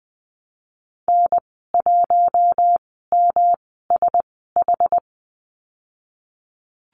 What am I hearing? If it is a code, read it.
Morse code, heard as a keyed tone, about 19 wpm, spaced as for 14 wpm, 700 Hz. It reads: N1MSH
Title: 231 Lesson 37g - Letter x - Call Signs 20wpm